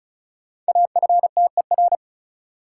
Morse code, heard as a keyed tone, about 36 wpm, 700 Hz. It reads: AFTER
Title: Sentences from Top 200 Words - New Word Drill - 35wpm